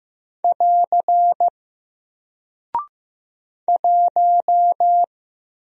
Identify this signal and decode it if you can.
Morse code, heard as a keyed tone, about 15 wpm, 700 Hz. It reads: <AR> 1
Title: Single Character (Letter:Number:Punctuation:Prosign) 15wpm